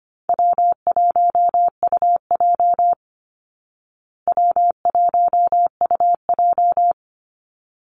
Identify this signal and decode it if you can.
Morse code, heard as a keyed tone, about 25 wpm, 700 Hz. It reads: W1UJ W1UJ
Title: Call Signs Repeated 3x - 25wpm